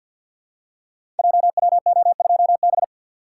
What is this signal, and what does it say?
Morse code, heard as a keyed tone, about 46 wpm, 700 Hz. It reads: JWO2B
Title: Code Groups - 5 Letter:Numbers - 50wpm